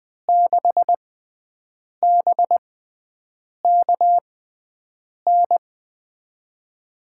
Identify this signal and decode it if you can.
Morse code, heard as a keyed tone, about 20 wpm, spaced as for 7 wpm, 700 Hz. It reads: 6BKN
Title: USA Call Signs - 6 Characters - 6x Character Spacing - 20wpm